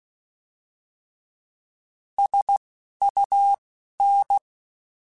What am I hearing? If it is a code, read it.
Morse code, heard as a keyed tone, about 16 wpm, spaced as for 11 wpm, 800 Hz. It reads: SUN
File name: Alphabetical / Alphabetical mp3s / Ss @10